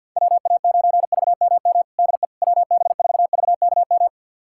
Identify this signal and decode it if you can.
Morse code, heard as a keyed tone, about 50 wpm, 700 Hz. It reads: WA9VMM DE WB4VKM
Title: USA Callsign DE Callsign - 6 Characters - 50wpm